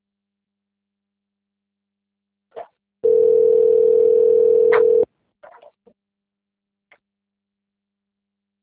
Ringback tone (North America), 440 + 480 Hz.
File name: out- - - - - .1856